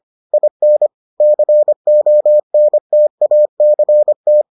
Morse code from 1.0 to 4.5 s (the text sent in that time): CONTACT